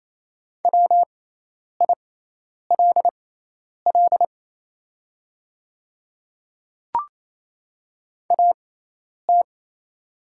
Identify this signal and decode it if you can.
Morse code, heard as a keyed tone, about 28 wpm, spaced as for 10 wpm, 690 Hz. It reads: WILL AT
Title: 6x Character Spacing - Top 100 Words - 28wpm